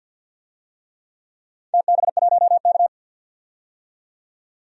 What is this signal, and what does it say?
Morse code, heard as a keyed tone, about 51 wpm, 700 Hz. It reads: TB1K
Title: DXCC-100 Call Signs - 4 Characters - 50wpm